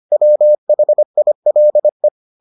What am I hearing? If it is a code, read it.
Morse code, heard as a keyed tone, about 25 wpm, 600 Hz. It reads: WHILE